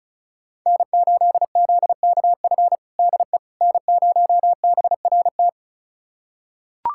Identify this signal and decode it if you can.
Morse code, heard as a keyed tone, about 36 wpm, 700 Hz. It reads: N8ZKF DE N0BRT
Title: USA Callsign DE Callsign - 5 Characters - 35wpm